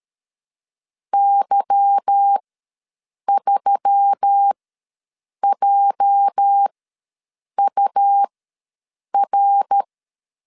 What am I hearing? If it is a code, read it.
Morse code, heard as a keyed tone, about 13 wpm, spaced as for 7 wpm, 800 Hz. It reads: Y3JUR
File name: LEZIONI / 8 YJ45 / 8 YJ ws 12wpm 3342